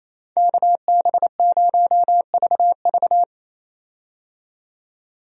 Morse code, heard as a keyed tone, about 28 wpm, 700 Hz. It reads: KB0VV
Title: USA Call Signs - 5 Characters - 28wpm